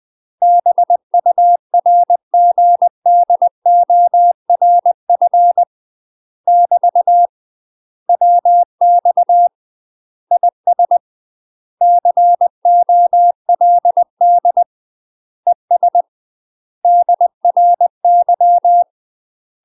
Morse code, heard as a keyed tone, about 20 wpm, 700 Hz. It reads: BURGDORF <BT> WX IS COLD ES DRY
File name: UK MASTER QSOs TRAINING / 20WPM / UK MASTER QSOs 0070